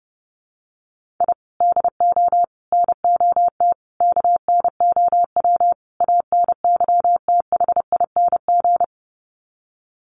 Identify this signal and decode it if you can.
Morse code, heard as a keyed tone, about 30 wpm, 700 Hz. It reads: I DO NOT KNOW ANYTHING
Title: Sets of 5 Words Encore 30wpm